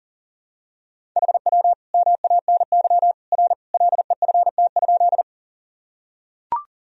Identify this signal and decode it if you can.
Morse code, heard as a keyed tone, about 40 wpm, 700 Hz. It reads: HW MANY R LEFT?
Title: Sentences from Top 300 Words and Abbreviations 40wpm